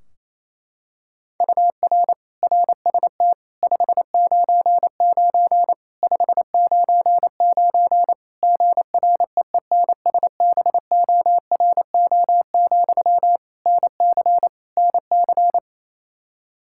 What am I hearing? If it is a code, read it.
Morse code, heard as a keyed tone, about 27 wpm, 700 Hz. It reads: UR RST 599 599 GREENSBORO, NC NC
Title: Practice QSO - Exchange 1 - Beginning - 28wpm